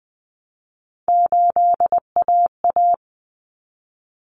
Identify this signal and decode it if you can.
Morse code, heard as a keyed tone, about 20 wpm, 700 Hz. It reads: 8AA
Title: Common Cut Numbers - 3-Digit 20wpm